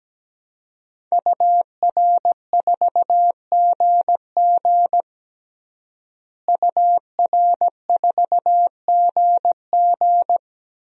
Morse code, heard as a keyed tone, about 17 wpm, 700 Hz. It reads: UR4GG UR4GG
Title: DXCC-100 Call Signs - 5 Characters - Repeated 3x - 17wpm